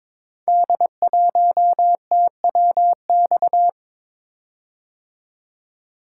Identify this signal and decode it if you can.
Morse code, heard as a keyed tone, about 23 wpm, 700 Hz. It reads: D1TWX
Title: Code Groups - 5 Letter:Numbers - 22wpm